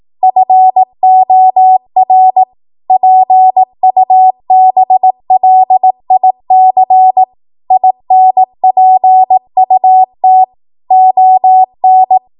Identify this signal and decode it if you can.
Morse code, heard as a keyed tone, about 18 wpm, 750 Hz. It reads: FOR PUBLIC INPUT ON
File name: Morse / 18WPM